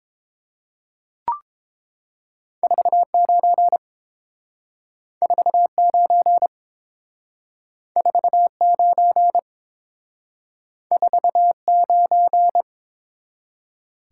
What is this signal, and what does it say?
Morse code, heard as a keyed tone, about 29 wpm, 700 Hz. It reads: 49 49 49 49